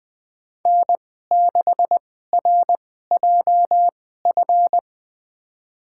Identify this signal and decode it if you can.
Morse code, heard as a keyed tone, about 20 wpm, spaced as for 14 wpm, 700 Hz. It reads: N6RJF
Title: USA Call Signs - 2x Character Spacing - 20wpm